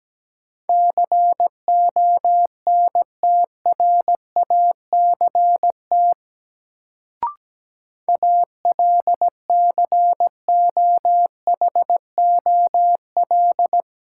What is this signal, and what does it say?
Morse code, heard as a keyed tone, about 17 wpm, 700 Hz. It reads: CONTRACT ALCOHOL
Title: Sentences from Top 1400 Words - New Word Drill - 17wpm